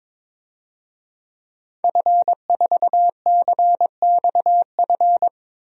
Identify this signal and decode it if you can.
Morse code, heard as a keyed tone, about 21 wpm, 700 Hz. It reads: F4CXF
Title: DXCC-100 Call Signs - 5 Characters - Repeated 3x - 22wpm